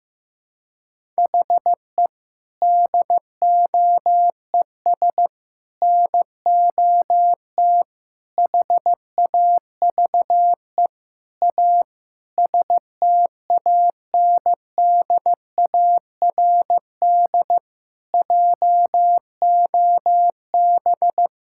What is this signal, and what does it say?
Morse code, heard as a keyed tone, about 15 wpm, 700 Hz. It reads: HE DOES NOT HAVE A STANDARD JOB
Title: Sentences from Top 1000 Words - Review Part 6 - 15wpm